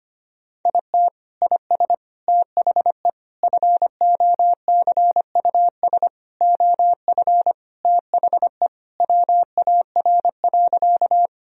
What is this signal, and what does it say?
Morse code, heard as a keyed tone, about 25 wpm, 700 Hz. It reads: IT IS THE FOCUS OF THE WAR.
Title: Sentences from Top 1000 Words - Review Part 4 - 25wpm